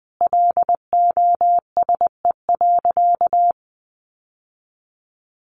Morse code, heard as a keyed tone, about 20 wpm, 700 Hz. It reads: LOSE.
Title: Sentences from Top 1100 Words 20wpm